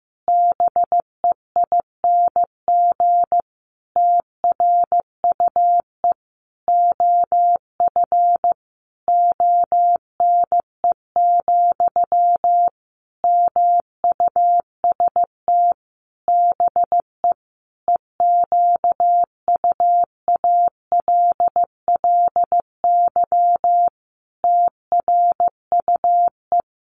Morse code, heard as a keyed tone, about 15 wpm, 700 Hz. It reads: BEING TRUE OF ONE, MUST BE EQUALLY TRUE
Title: Sentences from Top 1400 Words 15wpm